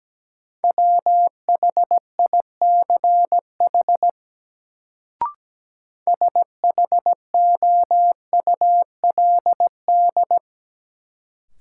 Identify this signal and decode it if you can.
Morse code, heard as a keyed tone, about 17 wpm, 700 Hz. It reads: WHICH SHOULD
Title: Sets of 2 Words - New Word Drill - 17wpm